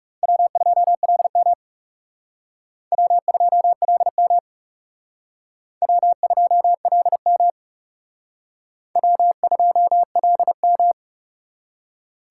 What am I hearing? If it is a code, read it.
Morse code, heard as a keyed tone, about 37 wpm, 700 Hz. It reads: W2LM W2LM W2LM W2LM
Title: Speed Racing - USA 1x2 Call Signs 30wpm